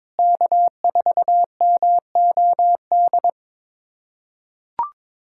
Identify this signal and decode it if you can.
Morse code, heard as a keyed tone, about 22 wpm, 700 Hz. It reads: K4MOD
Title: USA Callsign DE Callsign - 5 Characters - 22wpm